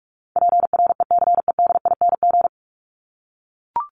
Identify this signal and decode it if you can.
Morse code, heard as a keyed tone, about 44 wpm, 700 Hz. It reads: PRECEDING
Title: ING Suffix - Top 100 Words with Base Words 45wpm